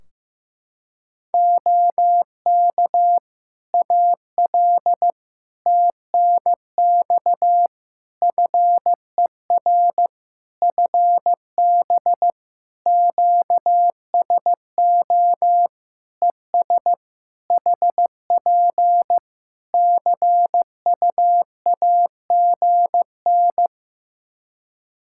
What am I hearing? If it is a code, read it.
Morse code, heard as a keyed tone, about 16 wpm, 700 Hz. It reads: OK AL TNX FER FB QSO ES HP CUAGN
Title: Practice QSO - Exchange 4 - Ending - 15wpm